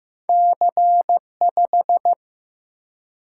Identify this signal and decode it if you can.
Morse code, heard as a keyed tone, about 15 wpm, 700 Hz. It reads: C5
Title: Single-Letter Single-Number 15wpm